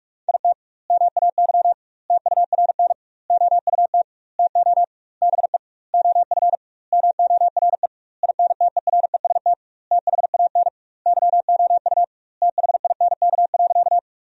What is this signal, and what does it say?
Morse code, heard as a keyed tone, about 42 wpm, 700 Hz. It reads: IT MAY TURN OUT TO BE OF MORE INTEREST THAN YOU THINK.
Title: 2x Word-Spacing - Sentences from Top 400 Words - 45wpm